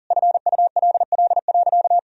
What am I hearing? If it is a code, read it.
Morse code, heard as a keyed tone, about 40 wpm, 700 Hz. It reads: FULL.
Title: Sentences from Top 1000 Words - Review Part 3 - 40wpm